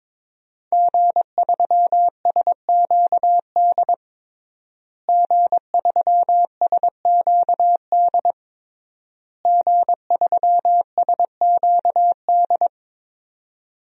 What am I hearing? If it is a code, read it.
Morse code, heard as a keyed tone, about 22 wpm, 700 Hz. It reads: G3SQD G3SQD G3SQD
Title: DXCC-100 Call Signs - 5 Characters - Repeated 3x - 22wpm